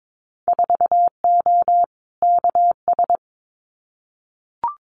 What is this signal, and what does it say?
Morse code, heard as a keyed tone, about 22 wpm, 700 Hz. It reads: 4O KS
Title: Winter Field Day Exchanges Encore 22wpm